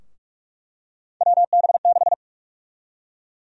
Morse code, heard as a keyed tone, about 46 wpm, 700 Hz. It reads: WB6
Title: USA Call Sign Prefixes - Repeated 2x - 45wpm